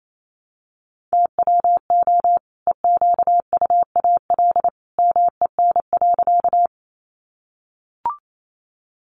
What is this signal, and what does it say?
Morse code, heard as a keyed tone, about 28 wpm, 700 Hz. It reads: TWO EQUAL MEN.
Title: Sentences from Top 1400 Words 28wpm